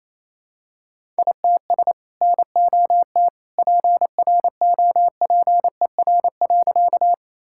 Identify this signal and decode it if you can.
Morse code, heard as a keyed tone, about 28 wpm, 700 Hz. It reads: ITS NOT PROPER.
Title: Sentences from Top 1000 Words - Review Part 5 - 28wpm